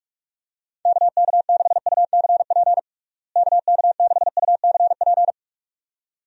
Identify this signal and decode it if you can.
Morse code, heard as a keyed tone, about 46 wpm, 700 Hz. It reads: KK6UCP KK6UCP
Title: USA Call Signs - 6 Characters - Repeated 2x - 45wpm